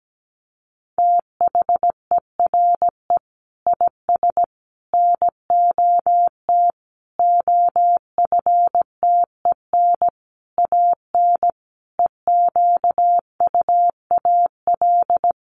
Morse code, heard as a keyed tone, about 17 wpm, 700 Hz. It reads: THERE IS NOT OFTEN AN EQUAL
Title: Sentences from Top 1400 Words 17wpm